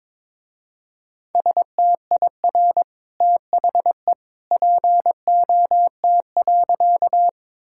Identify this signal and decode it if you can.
Morse code, heard as a keyed tone, about 21 wpm, 700 Hz. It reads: STIR THE POT.
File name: practice / 22wpm / Easy Sentences with Hints - 22wpm